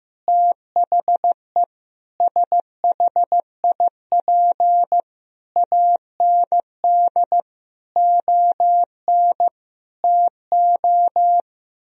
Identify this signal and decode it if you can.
Morse code, heard as a keyed tone, about 15 wpm, 700 Hz. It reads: THE SHIP AND ON TO
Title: Sentences from Top 1000 Words - Review Part 2 - 15wpm